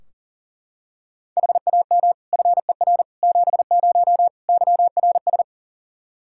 Morse code, heard as a keyed tone, about 40 wpm, 700 Hz. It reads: HAM FER 70 YRS
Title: Practice QSO - Exchange 3 - More Info - 40wpm